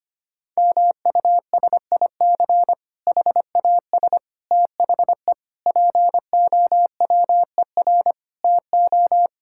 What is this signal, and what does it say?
Morse code, heard as a keyed tone, about 25 wpm, 700 Hz. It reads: MUSIC HAS THE POWER TO
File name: practice / 25wpm / Difficult Sentences with Hints - 25wpm